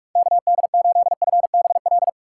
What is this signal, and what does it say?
Morse code, heard as a keyed tone, about 44 wpm, 700 Hz. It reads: KD8FBL
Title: USA Call Signs - 6 Characters - 6x Character Spacing - 45wpm